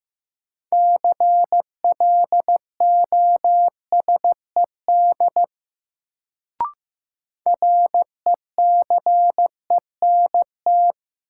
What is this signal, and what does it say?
Morse code, heard as a keyed tone, about 15 wpm, 700 Hz. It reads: CLOSED RECENT
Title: Sentences from Top 1400 Words - New Word Drill Alt - 15wpm